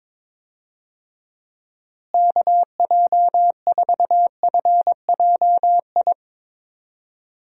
Morse code, heard as a keyed tone, about 22 wpm, 700 Hz. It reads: KJ4FJI
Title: USA Call Signs - 6 Characters - Repeated 2x - 22wpm